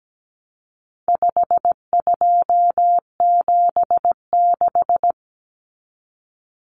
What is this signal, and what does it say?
Morse code, heard as a keyed tone, about 17 wpm, 700 Hz. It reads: 5276
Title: Numbers - 4-Digit 17wpm